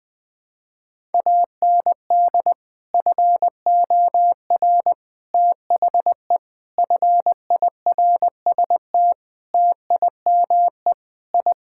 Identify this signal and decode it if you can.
Morse code, heard as a keyed tone, about 20 wpm, 700 Hz. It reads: AND FOR THE FIRST TIME I
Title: Sentences with 5-Word Phrases - 20wpm